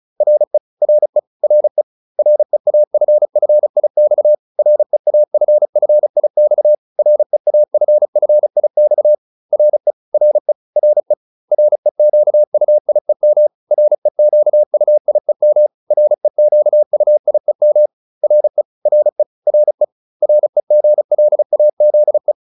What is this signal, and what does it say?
Morse code, heard as a keyed tone, about 35 wpm, 600 Hz. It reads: RE RE RE REAFFIX REAFFIX REAFFIX RE RE RE REQUIEM REQUIEM REQUIEM RE RE RE REGLAZE